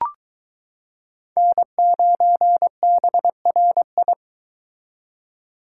Morse code, heard as a keyed tone, about 23 wpm, 700 Hz. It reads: N9BRI